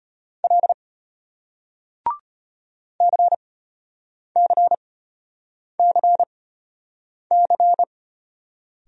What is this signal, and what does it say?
Morse code, heard as a keyed tone, about 34 wpm, spaced as for 8 wpm, 700 Hz. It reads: L CCCC